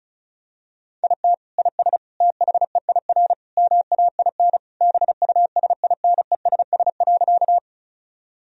Morse code, heard as a keyed tone, about 36 wpm, 700 Hz. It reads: IT IS THEIR MAIN BUSINESS.
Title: Sentences from Top 1000 Words - Review Part 2 - 35wpm